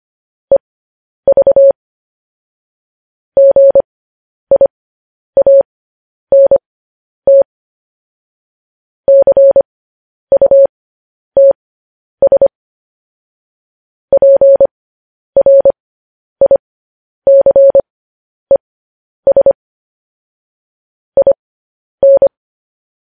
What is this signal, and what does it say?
Morse code, heard as a keyed tone, about 25 wpm, spaced as for 10 wpm, 550 Hz. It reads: EV GIANT CUTS PRICES IN